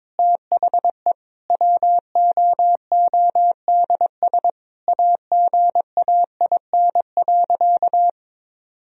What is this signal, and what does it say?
Morse code, heard as a keyed tone, about 21 wpm, 700 Hz. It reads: THE WOODS AGAIN.
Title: Sentences from Top 1000 Words - Review Part 2 - 22wpm